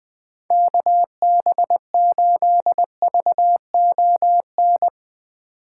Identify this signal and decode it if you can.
Morse code, heard as a keyed tone, about 20 wpm, 700 Hz. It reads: KB8VON
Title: 2023 POTA Hunter Call Signs - 20wpm